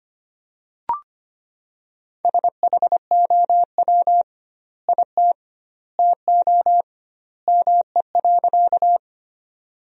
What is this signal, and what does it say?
Morse code, heard as a keyed tone, about 25 wpm, 700 Hz. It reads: SHOW IT TO ME.
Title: 2x Word-Spacing - Sentences from Top 200 Words - 25wpm